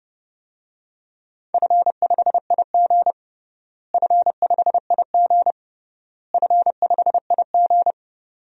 Morse code, heard as a keyed tone, about 32 wpm, 700 Hz. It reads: F5IG F5IG F5IG
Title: DXCC-100 Call Signs - 4 Characters - Repeated 3x - 30wpm